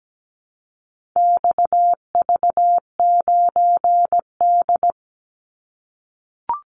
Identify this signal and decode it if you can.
Morse code, heard as a keyed tone, about 17 wpm, 700 Hz. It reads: XV9D
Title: Call Signs - 17wpm